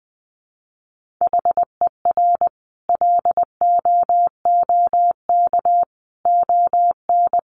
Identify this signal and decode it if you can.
Morse code, heard as a keyed tone, about 20 wpm, 700 Hz. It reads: HER LOOK ON
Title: Sentences from Top 1400 Words 20wpm